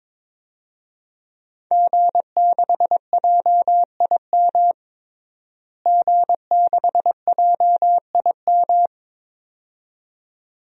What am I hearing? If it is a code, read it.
Morse code, heard as a keyed tone, about 22 wpm, 700 Hz. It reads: G6JIM G6JIM
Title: DXCC-100 Call Signs - 5 Characters - Repeated 2x - 22wpm